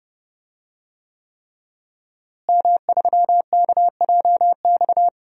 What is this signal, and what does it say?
Morse code, heard as a keyed tone, about 31 wpm, 700 Hz. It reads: M3KJX
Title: DXCC-100 Call Signs - 5 Characters - Repeated 3x - 30wpm